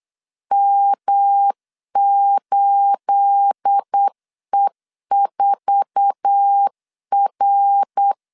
Morse code, heard as a keyed tone, about 8 wpm, 800 Hz. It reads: M8E4R
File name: LEZIONI / 9 28 8wpm 3512